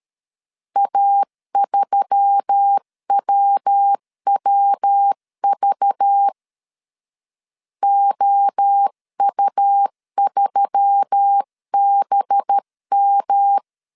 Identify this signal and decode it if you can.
Morse code, heard as a keyed tone, about 13 wpm, 800 Hz. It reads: A3WWV OU3BM
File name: CW / DET / 1 6 1500C S C 12wpm 3533